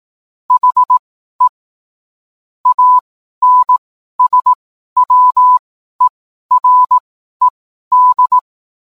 Morse code, heard as a keyed tone, about 18 wpm, spaced as for 12 wpm, 1000 Hz. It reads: HE ANSWERED